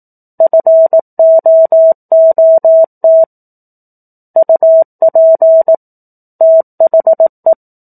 Morse code, heard as a keyed tone, about 18 wpm, 650 Hz. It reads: FOOT UP THE
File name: 04-Chapter-04 18 WPM